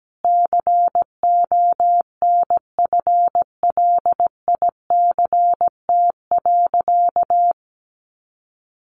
Morse code, heard as a keyed tone, about 17 wpm, 700 Hz. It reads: CONFLICT.
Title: Sentences from Top 1300 Words 17wpm